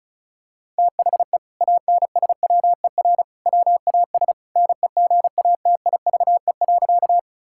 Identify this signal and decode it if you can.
Morse code, heard as a keyed tone, about 36 wpm, 700 Hz. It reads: THE ANSWER WAS NEGATIVE.
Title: Sentences from Top 1000 Words - Review Part 6 - 35wpm